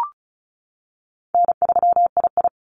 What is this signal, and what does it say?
Morse code, heard as a keyed tone, about 36 wpm, 700 Hz. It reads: N3II
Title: NA SOTA Activator Callsigns 35wpm